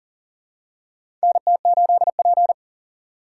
Morse code, heard as a keyed tone, about 39 wpm, 700 Hz. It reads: NT8P
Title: USA Call Signs - 4 Characters - 3x Character Spacing - 40wpm